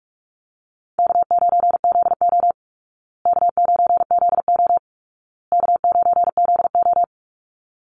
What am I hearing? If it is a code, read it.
Morse code, heard as a keyed tone, about 44 wpm, 700 Hz. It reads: K9ZO K9ZO K9ZO
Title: Easy Call Signs Repeated 3x - 45wpm